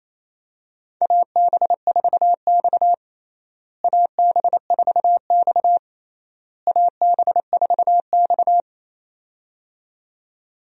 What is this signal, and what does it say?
Morse code, heard as a keyed tone, about 28 wpm, 700 Hz. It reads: AB4X AB4X AB4X
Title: USA Call Signs - 4 Characters - Repeated 3x - 28wpm